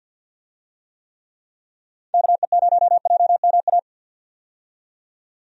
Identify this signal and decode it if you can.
Morse code, heard as a keyed tone, about 49 wpm, 700 Hz. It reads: KE0JMA
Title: USA Call Signs - 6 Characters - 50wpm